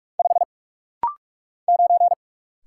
Morse code, heard as a keyed tone, about 46 wpm, 700 Hz. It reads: 5 9
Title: Single Letter-Number - Rapid-Fire 45wpm